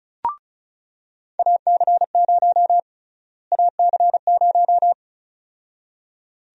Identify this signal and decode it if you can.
Morse code, heard as a keyed tone, about 36 wpm, 700 Hz. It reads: AC0 AC0
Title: USA Call Sign Prefixes - Repeated 2x - 35wpm